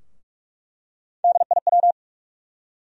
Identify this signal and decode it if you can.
Morse code, heard as a keyed tone, about 41 wpm, 700 Hz. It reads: DIW
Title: USA Call Sign Postfixes - 3 Letters 45wpm